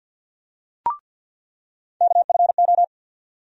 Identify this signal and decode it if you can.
Morse code, heard as a keyed tone, about 47 wpm, 700 Hz. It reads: KFO
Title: USA Call Sign Postfixes - 3 Letters 50wpm